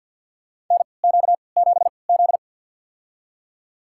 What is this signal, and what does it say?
Morse code, heard as a keyed tone, about 48 wpm, spaced as for 28 wpm, 700 Hz. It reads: NQ7Z
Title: USA Call Signs - 4 Characters - 3x Character Spacing - 50wpm